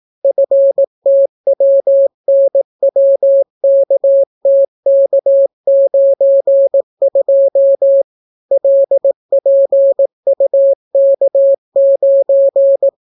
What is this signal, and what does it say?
Morse code, heard as a keyed tone, about 18 wpm, 550 Hz. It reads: FTWNWKTK92 LPUK9